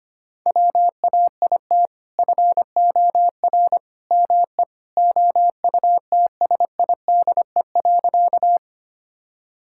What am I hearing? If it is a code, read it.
Morse code, heard as a keyed tone, about 25 wpm, 700 Hz. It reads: WAIT FOR ME OUTSIDE.
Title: Sentences from Top 1000 Words - Review Part 6 - 25wpm